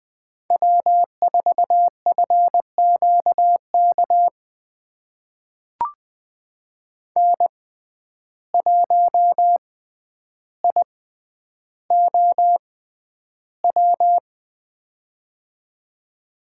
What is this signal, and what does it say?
Morse code, heard as a keyed tone, about 20 wpm, spaced as for 10 wpm, 700 Hz. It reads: W4FQK N1IOW
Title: USA Call Signs - 6x Character Spacing - 20wpm